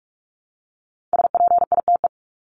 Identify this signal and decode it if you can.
Morse code, heard as a keyed tone, about 44 wpm, 700 Hz. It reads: SPITE
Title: Top 1751-2000 Words 45wpm